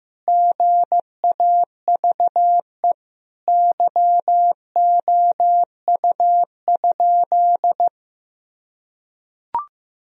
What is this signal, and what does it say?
Morse code, heard as a keyed tone, about 15 wpm, 700 Hz. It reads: GAVE YOU?
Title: Sentences from Top 1000 Words - Review Part 4 - 15wpm